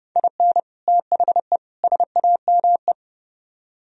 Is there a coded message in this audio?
Yes: Morse code from 0.2 to 2.9 s, heard as a keyed tone, about 28 wpm, 700 Hz. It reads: IN THE SAME